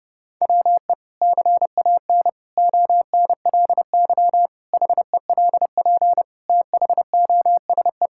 Morse code, heard as a keyed tone, about 30 wpm, 700 Hz. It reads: WE CAN ONLY HELP THOSE
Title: Sentences from Top 1000 Words - Review Part 5 - 30wpm